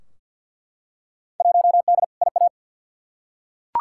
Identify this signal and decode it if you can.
Morse code, heard as a keyed tone, about 50 wpm, 700 Hz. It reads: 1D IA
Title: Field Day Exchanges - 50wpm